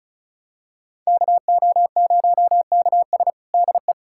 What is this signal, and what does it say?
Morse code, heard as a keyed tone, about 36 wpm, 700 Hz. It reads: KO0KS DE
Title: USA Callsign DE Callsign 35wpm